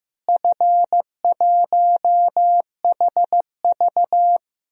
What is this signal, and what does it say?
Morse code, heard as a keyed tone, about 15 wpm, 700 Hz. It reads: F1HV
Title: DXCC-100 Call Signs - 4 Characters - Repeated 3x - 15wpm